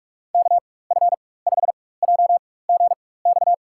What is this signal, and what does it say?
Morse code, heard as a keyed tone, about 46 wpm, spaced as for 21 wpm, 700 Hz. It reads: KF5JGX
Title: USA Call Signs - 6 Characters - 4x Character Spacing - 45wpm